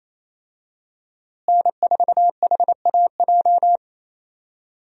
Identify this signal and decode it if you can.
Morse code, heard as a keyed tone, about 28 wpm, 700 Hz. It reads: N4HAJ
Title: USA Call Signs - 5 Characters - Repeated 3x - 28wpm